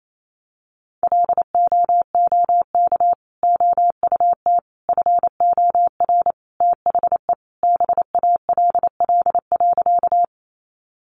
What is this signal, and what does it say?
Morse code, heard as a keyed tone, about 28 wpm, 700 Hz. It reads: LOOK OUT FOR THE BALL.
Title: Sentences from Top 900 Words 28wpm